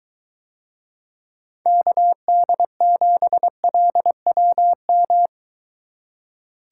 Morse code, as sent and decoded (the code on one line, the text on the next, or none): -.- -.. --... .-.. .-- --
KD7LWM